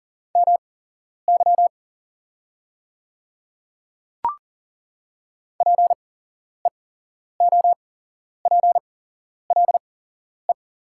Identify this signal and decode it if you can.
Morse code, heard as a keyed tone, about 40 wpm, spaced as for 11 wpm, 690 Hz. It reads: MY PEOPLE
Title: 8x Character Spacing - Top 200 Words - 40wpm